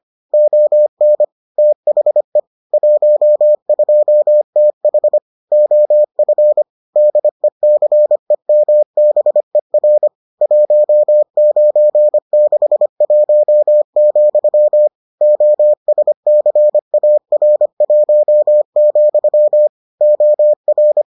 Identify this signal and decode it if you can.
Morse code, heard as a keyed tone, about 25 wpm, 600 Hz. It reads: ON THE 12TH OF DECEMBER 1961, OSCAR1, OR